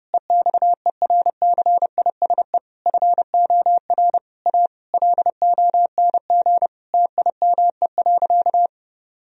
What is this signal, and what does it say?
Morse code, heard as a keyed tone, about 32 wpm, 700 Hz. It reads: EXERCISE FOR A LONG TIME.
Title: Sentences from Top 1000 Words - Review Part 4 - 30wpm